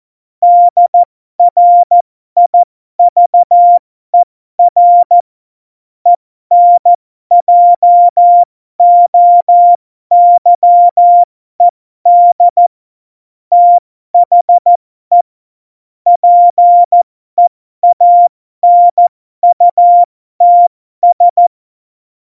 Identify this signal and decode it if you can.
Morse code, heard as a keyed tone, about 14 wpm, spaced as for 12 wpm, 700 Hz. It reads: DRIVER ENJOYED THE PEANUTS
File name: Morsekurs / Witze